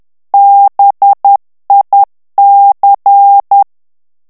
Morse code, heard as a keyed tone, about 10 wpm, 800 Hz. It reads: BIC
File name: morse mp3 / ITALIEN texte clair 1 vitesse 10 mots minute